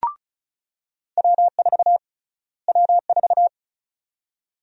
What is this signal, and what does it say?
Morse code, heard as a keyed tone, about 36 wpm, 700 Hz. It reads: W4 W4
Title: USA Call Sign Prefixes - Repeated 2x - 35wpm